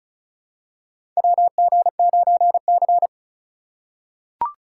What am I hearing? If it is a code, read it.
Morse code, heard as a keyed tone, about 34 wpm, 690 Hz. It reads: WG9C
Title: USA Call Signs - 4 Characters - 2x Character Spacing - 35wpm